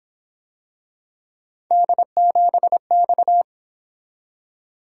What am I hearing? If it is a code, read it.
Morse code, heard as a keyed tone, about 26 wpm, 700 Hz. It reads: D7X
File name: Speed-Racing - Three Letter-Number Combinations - 22wpm